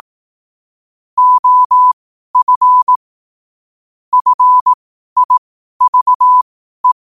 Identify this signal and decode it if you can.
Morse code, heard as a keyed tone, about 18 wpm, spaced as for 12 wpm, 1000 Hz. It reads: OF FIVE
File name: qotdfiles / 12wpm